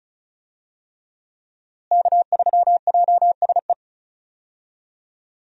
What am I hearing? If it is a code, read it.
Morse code, heard as a keyed tone, about 35 wpm, 700 Hz. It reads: K3JSE
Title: USA Call Signs - 5 Characters - Repeated 3x - 35wpm